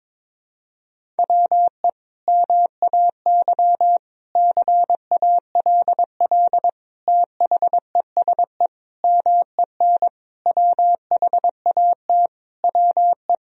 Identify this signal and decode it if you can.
Morse code, heard as a keyed tone, about 22 wpm, 700 Hz. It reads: WE MAY CALL THESE MEN WHAT WE
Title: Sentences from Top 1000 Words - Review Part 1 - 22wpm